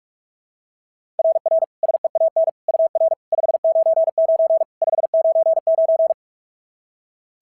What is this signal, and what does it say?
Morse code, heard as a keyed tone, about 44 wpm, 690 Hz. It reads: RR SEAN UR 599 599
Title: SOTA QSO Practice 45wpm